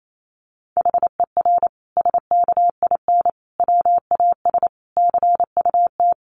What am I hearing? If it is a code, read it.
Morse code, heard as a keyed tone, about 28 wpm, 700 Hz. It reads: HER SKIN WAS CUT
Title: Sentences from Top 1300 Words 28wpm